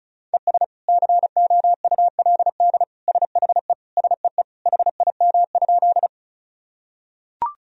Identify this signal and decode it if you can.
Morse code, heard as a keyed tone, about 34 wpm, 700 Hz. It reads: ES COULD SHE SEE HIM?
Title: Sentences from Top 100 Words and Abbreviations 35wpm